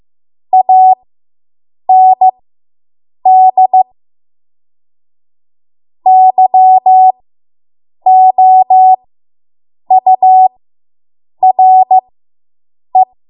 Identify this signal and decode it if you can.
Morse code, heard as a keyed tone, about 15 wpm, spaced as for 7 wpm, 750 Hz. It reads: AND YOURE